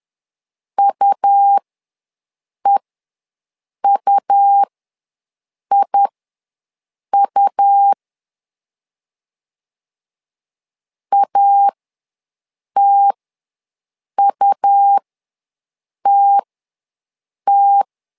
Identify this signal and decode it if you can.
Morse code, heard as a keyed tone, about 11 wpm, spaced as for 6 wpm, 800 Hz. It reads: UEUIU ATUTT
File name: LEZIONI / 2 UHDN / 2 U ws 10wpm 2017